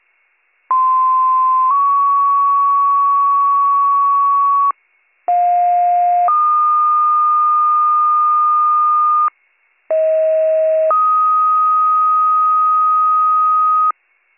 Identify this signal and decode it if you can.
Two-tone page, heard as a one-second tone, then a three-second tone, about 1040 then 1120 Hz, about 710 then 1220 Hz, about 640 then 1250 Hz. It is heard from 0.7 to 13.9 s.